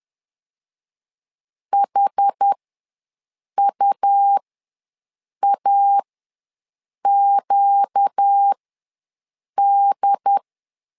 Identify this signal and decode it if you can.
Morse code, heard as a keyed tone, about 10 wpm, spaced as for 6 wpm, 800 Hz. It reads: HUAQD
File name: LEZIONI / 4 Q ws 10wpm 2306